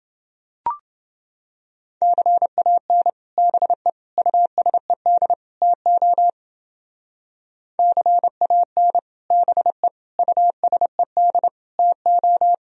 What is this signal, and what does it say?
Morse code, heard as a keyed tone, about 28 wpm, 700 Hz. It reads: CAN BE USED TO CAN BE USED TO